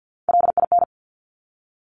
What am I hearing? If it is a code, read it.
Morse code, heard as a keyed tone, about 50 wpm, 700 Hz. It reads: LIN
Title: Top Three Letter Combinations in Words 50wpm